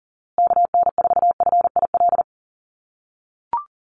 Morse code, heard as a keyed tone, about 40 wpm, 700 Hz. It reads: KN4FIL
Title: USA Call Signs Repeated 3x 40wpm